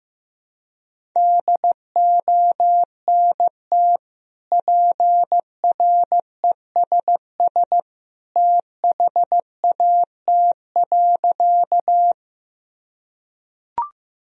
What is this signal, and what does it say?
Morse code, heard as a keyed tone, about 15 wpm, 700 Hz. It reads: DONT PRESS THAT.
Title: Sentences from Top 1000 Words - Review Part 1 - 15wpm